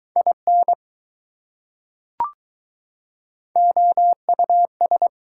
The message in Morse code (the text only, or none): IN OUS